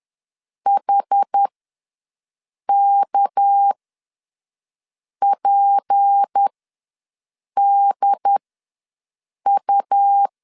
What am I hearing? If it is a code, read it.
Morse code, heard as a keyed tone, about 10 wpm, spaced as for 5 wpm, 800 Hz. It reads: HKPDU
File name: LEZIONI / 6 PKX0 / 6 PKX0 ws 10wpm 4220